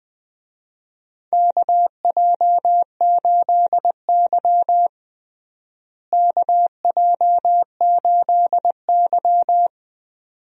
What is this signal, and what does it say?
Morse code, heard as a keyed tone, about 20 wpm, 700 Hz. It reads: KJ8Y KJ8Y
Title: USA Call Signs - 4 Characters - Repeated 2x - 20wpm